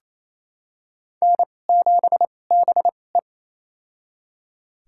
Morse code, heard as a keyed tone, about 27 wpm, spaced as for 20 wpm, 690 Hz. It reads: N7BE
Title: USA Call Signs - 4 Characters - 2x Character Spacing - 28wpm